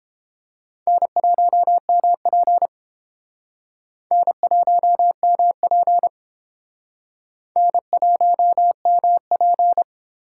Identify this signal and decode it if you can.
Morse code, heard as a keyed tone, about 31 wpm, 700 Hz. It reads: N1MP N1MP N1MP